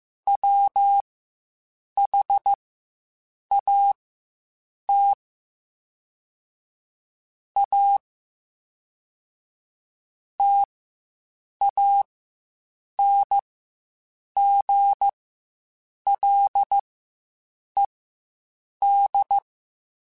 Morse code, heard as a keyed tone, about 15 wpm, spaced as for 7 wpm, 800 Hz. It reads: WHAT A TANGLED
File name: Quotes Spaced / Quotes Spaced-mp3s / Deceive sp @10